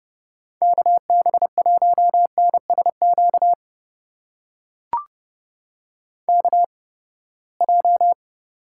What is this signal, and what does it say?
Morse code, heard as a keyed tone, about 31 wpm, spaced as for 21 wpm, 700 Hz. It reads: KB1NSQ KJ
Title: USA Call Signs - 6 Characters - 8x Character Spacing - 30wpm